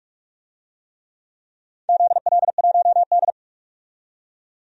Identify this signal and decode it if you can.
Morse code, heard as a keyed tone, about 47 wpm, 700 Hz. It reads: ZL1D
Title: DXCC-100 Call Signs - 4 Characters - 45wpm